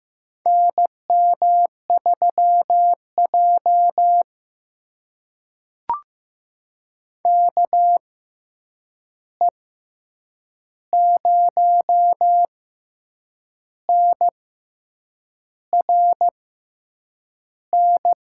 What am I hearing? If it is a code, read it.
Morse code, heard as a keyed tone, about 15 wpm, spaced as for 7 wpm, 700 Hz. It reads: NM3J KE0NRN
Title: USA Call Signs - 6x Character Spacing - 15wpm